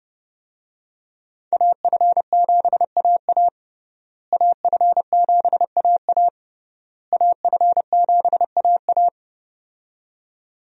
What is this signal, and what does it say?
Morse code, heard as a keyed tone, about 32 wpm, 700 Hz. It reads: AF7AA AF7AA AF7AA
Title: USA Call Signs - 5 Characters - Repeated 3x - 30wpm